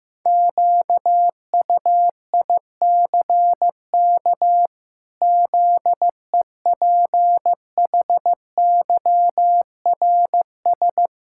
Morse code, heard as a keyed tone, about 16 wpm, 700 Hz. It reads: QUICK ZEPHYRS
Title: Pangram Challenge v2 - 15wpm